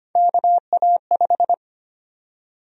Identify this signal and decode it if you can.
Morse code, heard as a keyed tone, about 25 wpm, 700 Hz. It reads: KA5
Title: USA Call Sign Prefixes - Repeated 2x - 25wpm